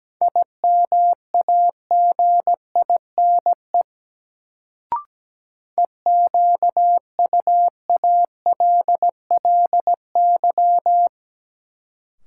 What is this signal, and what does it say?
Morse code, heard as a keyed tone, about 17 wpm, 700 Hz. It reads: IMAGINE EQUALLY
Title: Sentences from Top 1400 Words - New Word Drill - 17wpm